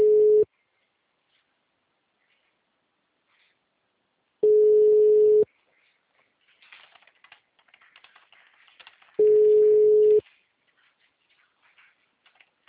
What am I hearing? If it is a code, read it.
Ringback tone (Europe, 425 Hz).
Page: Presidente Prudente - SP